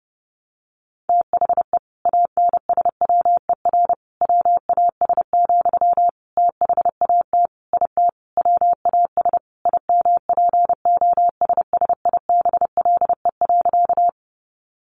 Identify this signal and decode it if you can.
Morse code, heard as a keyed tone, about 30 wpm, 700 Hz. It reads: THE ANSWER WAS, THAT IT WAS IMPOSSIBLE.
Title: Sentences from Top 1100 Words 30wpm